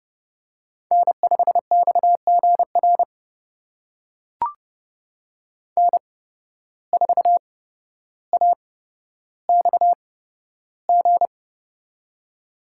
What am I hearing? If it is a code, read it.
Morse code, heard as a keyed tone, about 30 wpm, spaced as for 13 wpm, 700 Hz. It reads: N5XGR N4AXG
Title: USA Call Signs - 5 Characters - 8x Character Spacing - 30wpm